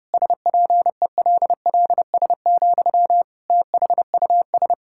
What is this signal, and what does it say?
Morse code, heard as a keyed tone, about 32 wpm, 700 Hz. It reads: SPELLS, THUS
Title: Pangram Challenge v2 - 30wpm